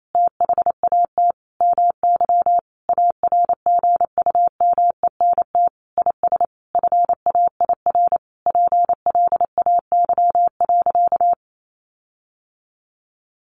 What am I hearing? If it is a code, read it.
Morse code, heard as a keyed tone, about 28 wpm, 700 Hz. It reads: THAT MY ARGUMENT IS FAIR PLAY.
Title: Sentences from Top 1400 Words 28wpm